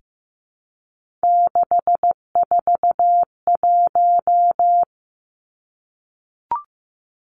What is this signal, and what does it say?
Morse code, heard as a keyed tone, about 15 wpm, 700 Hz. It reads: 641
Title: Three Challenging Letter-Number Combinations 15wpm